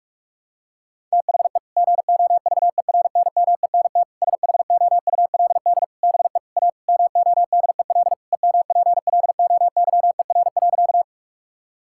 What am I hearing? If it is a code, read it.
Morse code, heard as a keyed tone, about 42 wpm, 700 Hz. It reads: THE GOVERNMENT SHOULD BE A MODEL EMPLOYER.
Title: Sentences from Top 1600 Words 45wpm